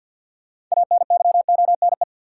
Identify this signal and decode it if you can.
Morse code, heard as a keyed tone, about 48 wpm, 700 Hz. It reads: ANYONE
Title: Sets of 5 Words Encore - New Word Drill - 50wpm